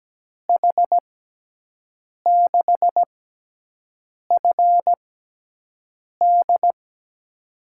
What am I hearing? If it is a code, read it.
Morse code, heard as a keyed tone, about 17 wpm, spaced as for 6 wpm, 700 Hz. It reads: H6FD
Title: USA Call Signs - 5 Characters - 6x Character Spacing - 17wpm